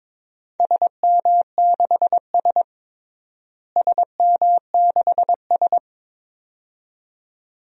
Morse code, heard as a keyed tone, about 23 wpm, 700 Hz. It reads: SM6S SM6S
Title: DXCC-100 Call Signs - 4 Characters - Repeated 2x - 22wpm